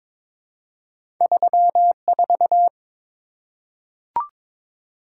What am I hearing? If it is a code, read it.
Morse code, heard as a keyed tone, about 22 wpm, 700 Hz. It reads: 34
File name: practice / 22wpm / Two Letter-Number Combinations - 22wpm